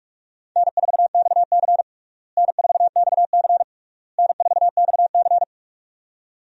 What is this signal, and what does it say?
Morse code, heard as a keyed tone, about 46 wpm, 700 Hz. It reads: N4XC N4XC N4XC
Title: USA Call Signs - 4 Characters - Repeated 3x - 45wpm